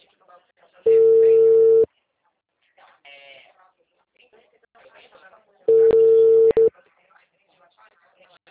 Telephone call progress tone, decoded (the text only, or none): ringback tone